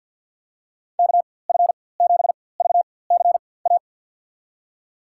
Morse code, heard as a keyed tone, about 49 wpm, spaced as for 23 wpm, 700 Hz. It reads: KF7VCA
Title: USA Call Signs - 6 Characters - 4x Character Spacing - 50wpm